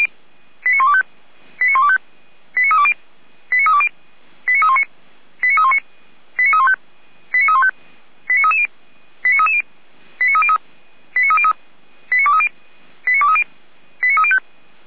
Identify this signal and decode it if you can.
Five-tone selective call (ZVEI), heard as quick runs of five tones, tones of about 80 ms. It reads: 89125 89125 89320 89320 89319 89319 89315 89315 89330 89330 89303 89303 89230 89230 89305
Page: SDR/ZVEI